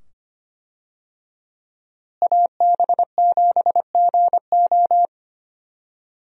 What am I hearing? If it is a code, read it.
Morse code, heard as a keyed tone, about 25 wpm, 700 Hz. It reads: AB7GO
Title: USA Call Signs - 5 Characters - Repeated 2x - 25wpm